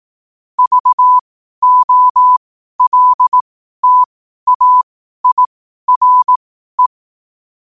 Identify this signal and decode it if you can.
Morse code, heard as a keyed tone, about 18 wpm, spaced as for 12 wpm, 1000 Hz. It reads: VOLTAIRE
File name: qotdfiles / 12wpm